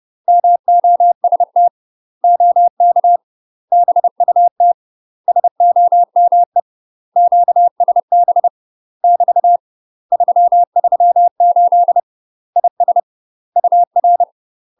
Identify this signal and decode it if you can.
Morse code, heard as a keyed tone, about 30 wpm, 700 Hz. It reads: MOST OK BUT SOME QSB <BT> 338 IS UR
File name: UK MASTER QSOs TRAINING / 30WPM / UK MASTER QSOs 0009